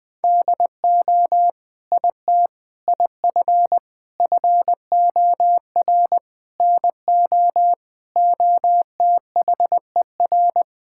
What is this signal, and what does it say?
Morse code, heard as a keyed tone, about 20 wpm, 700 Hz. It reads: DO IT IF FOR NO OTHER
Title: Sentences with 7-Word Phrases - Spoken Beginning - 20wpm